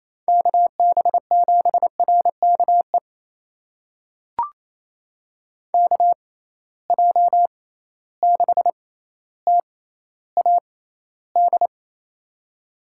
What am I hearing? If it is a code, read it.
Morse code, heard as a keyed tone, about 28 wpm, spaced as for 14 wpm, 700 Hz. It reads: KB7RKE KJ6TAD
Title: USA Call Signs - 6 Characters - 6x Character Spacing - 28wpm